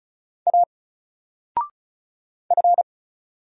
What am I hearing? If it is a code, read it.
Morse code, heard as a keyed tone, about 34 wpm, 700 Hz. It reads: A F
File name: Single-Letter-Number-35wpm